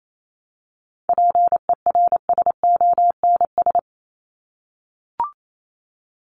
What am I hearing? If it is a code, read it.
Morse code, heard as a keyed tone, about 28 wpm, 700 Hz. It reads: PERSONS
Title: ES:S Suffix - Top 100 Words with Base Words 28wpm